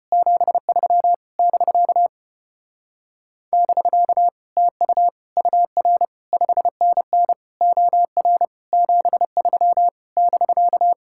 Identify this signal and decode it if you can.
Morse code, heard as a keyed tone, about 33 wpm, 700 Hz. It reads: 73 <BK> <BK> TU UR 5NN OR 73 <BK>